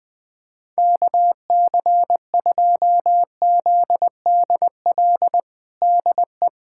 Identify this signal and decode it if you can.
Morse code, heard as a keyed tone, about 20 wpm, 700 Hz. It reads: KC2ZDL DE
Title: USA Callsign DE Callsign 20wpm